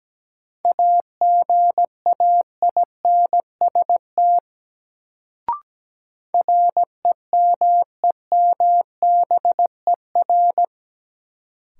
Morse code, heard as a keyed tone, about 17 wpm, 700 Hz. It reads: AGAINST REMEMBER
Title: Sentences from Top 400 Words - New Word Drill - 17wpm